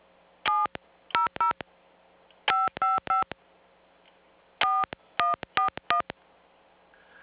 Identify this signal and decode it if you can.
DTMF, heard as telephone touch tones, tones of about 150 ms. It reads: 700 555 4141